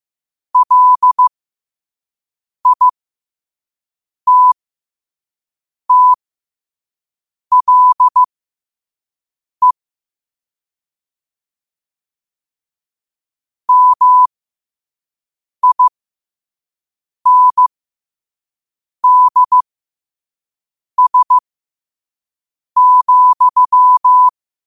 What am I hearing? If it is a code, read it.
Morse code, heard as a keyed tone, about 15 wpm, spaced as for 5 wpm, 1000 Hz. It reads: LITTLE MINDS,
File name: qotdfiles / 5wpm